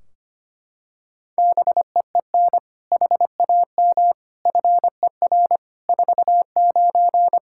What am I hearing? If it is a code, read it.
Morse code, heard as a keyed tone, about 25 wpm, 700 Hz. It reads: BEEN HAM FER 49
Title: Practice QSO - Exchange 3 - More Info - 25wpm